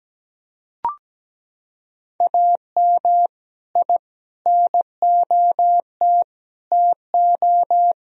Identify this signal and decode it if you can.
Morse code, heard as a keyed tone, about 17 wpm, 700 Hz. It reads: AM I NOT TO
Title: Sentences from Top 1000 Words - Review Part 2 - 17wpm